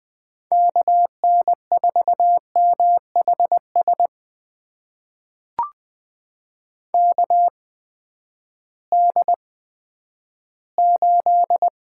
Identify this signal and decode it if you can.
Morse code, heard as a keyed tone, about 20 wpm, spaced as for 11 wpm, 700 Hz. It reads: KN4MHS KD8
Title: USA Call Signs - 6 Characters - 8x Character Spacing - 20wpm